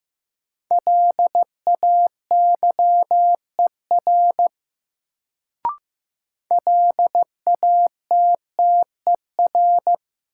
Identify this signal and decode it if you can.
Morse code, heard as a keyed tone, about 15 wpm, 700 Hz. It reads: LAYER LATTER
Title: Sentences from Top 1600 Words - New Word Drill - 15wpm